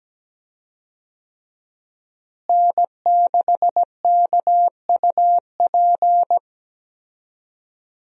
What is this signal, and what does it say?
Morse code, heard as a keyed tone, about 17 wpm, 700 Hz. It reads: N6KUP
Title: USA Call Signs - 5 Characters - Repeated 3x - 17wpm